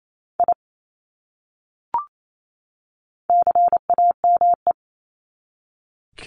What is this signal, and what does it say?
Morse code, heard as a keyed tone, about 28 wpm, 700 Hz. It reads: I CAME
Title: Top 300 Words 28wpm